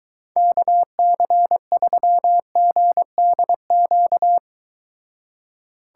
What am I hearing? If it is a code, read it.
Morse code, heard as a keyed tone, about 23 wpm, 700 Hz. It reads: KC3GDQ